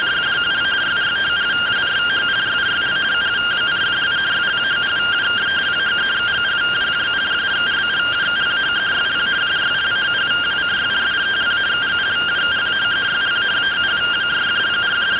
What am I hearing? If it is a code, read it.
Two-tone teleprinter signal, about 170 Hz shift.